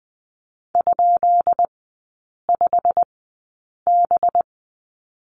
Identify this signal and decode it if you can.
Morse code, heard as a keyed tone, about 20 wpm, spaced as for 8 wpm, 690 Hz. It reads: ?5B
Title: 212 Lesson 34e - Number 8 - Three Characters 20wpm